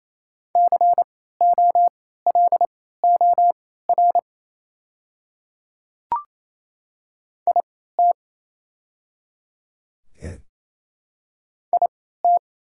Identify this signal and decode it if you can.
Morse code, heard as a keyed tone, about 27 wpm, spaced as for 16 wpm, 700 Hz. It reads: COLOR IT IT
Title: 3x Character Spacing - Top 300 Words - 28wpm